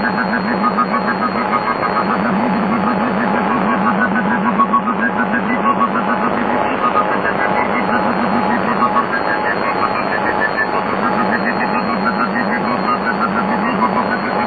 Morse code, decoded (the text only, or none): none